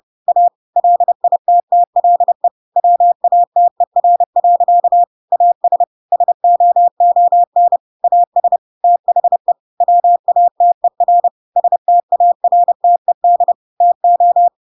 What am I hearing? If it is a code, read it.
Morse code, heard as a keyed tone, about 30 wpm, 700 Hz. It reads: A LITTLE WATER. AS SOON AS THE WATER STARTED TO
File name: Pinocchio / Pinocchio ch5 30 30 ik1hge